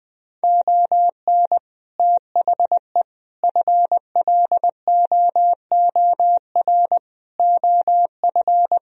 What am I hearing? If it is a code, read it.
Morse code, heard as a keyed tone, about 20 wpm, 700 Hz. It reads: ON THE FLOOR OF
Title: Speed-Racing - Sets of 5 Words 20wpm